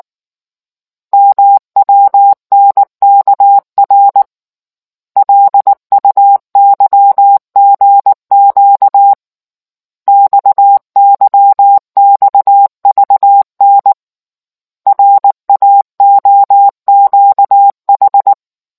Morse code, heard as a keyed tone, about 19 wpm, 800 Hz. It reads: MWNKR LUYGQ XYXVN RAOQH